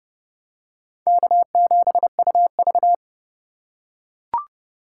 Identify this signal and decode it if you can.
Morse code, heard as a keyed tone, about 30 wpm, 700 Hz. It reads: K7UV
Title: USA Call Signs - 4 Characters - 4x Character Spacing - 30wpm